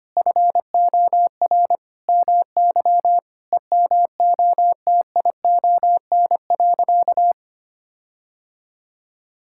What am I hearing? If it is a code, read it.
Morse code, heard as a keyed tone, about 25 wpm, 700 Hz. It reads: FOR MY EMOTION.
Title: Sentences from Top 1600 Words 25wpm